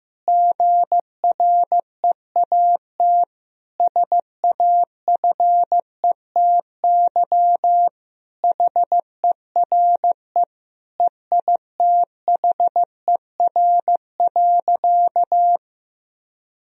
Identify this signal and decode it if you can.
Morse code, heard as a keyed tone, about 15 wpm, 700 Hz. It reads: GREAT SAFETY HERE EITHER.
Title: Sentences from Top 1000 Words - Review Part 4 - 15wpm